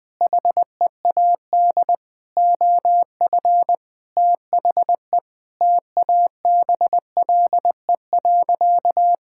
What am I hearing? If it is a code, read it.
Morse code, heard as a keyed tone, about 20 wpm, 700 Hz. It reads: HEAD OF THE TABLE.
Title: Sentences with 5-Word Phrases - 20wpm